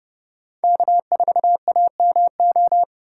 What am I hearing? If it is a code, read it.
Morse code, heard as a keyed tone, about 32 wpm, 700 Hz. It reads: K4AMO